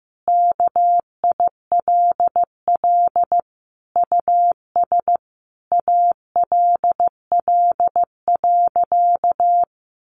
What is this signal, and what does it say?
Morse code, heard as a keyed tone, about 15 wpm, 700 Hz. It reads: KILL US ALL.
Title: Sentences from Top 1400 Words 15wpm